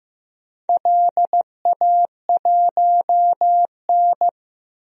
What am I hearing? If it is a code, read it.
Morse code, heard as a keyed tone, about 15 wpm, 700 Hz. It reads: LA1N
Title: DXCC-100 Call Signs - 4 Characters - Repeated 2x - 15wpm